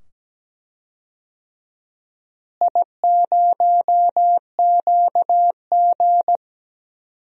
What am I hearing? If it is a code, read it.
Morse code, heard as a keyed tone, about 17 wpm, 700 Hz. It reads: I0QG
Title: DXCC-100 Call Signs - 4 Characters - Repeated 2x - 17wpm